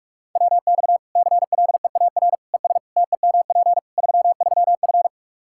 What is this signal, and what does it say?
Morse code, heard as a keyed tone, about 46 wpm, 700 Hz. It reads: WX CLEAR ES TEMP 33F
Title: Practice QSO - Exchange 2 - Info - 45wpm